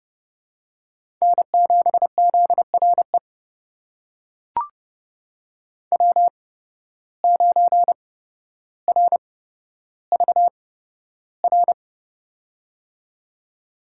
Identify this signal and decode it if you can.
Morse code, heard as a keyed tone, about 29 wpm, spaced as for 13 wpm, 700 Hz. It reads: N7ZRE W9RVR
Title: USA Call Signs - 5 Characters - 8x Character Spacing - 30wpm